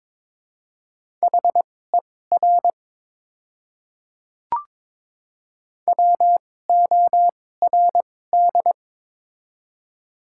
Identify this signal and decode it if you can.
Morse code, heard as a keyed tone, about 22 wpm, spaced as for 16 wpm, 700 Hz. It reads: HER WORD
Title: 2x Character Spacing - Top 300 Words - 22wpm